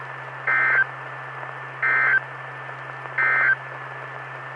Emergency Alert System (SAME) tones, heard as harsh buzzing bursts. It is heard from 0.4 to 3.6 s.
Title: Saline, AK SKYWARN Repeater System - W5RHS